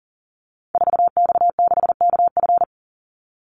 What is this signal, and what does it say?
Morse code, heard as a keyed tone, about 40 wpm, 700 Hz. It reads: 4X6KF
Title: Call Signs Repeated 2x - 40wpm